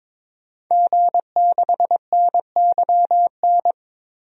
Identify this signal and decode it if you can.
Morse code, heard as a keyed tone, about 22 wpm, 700 Hz. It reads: G6NYN
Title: DXCC-100 Call Signs - 5 Characters - 22wpm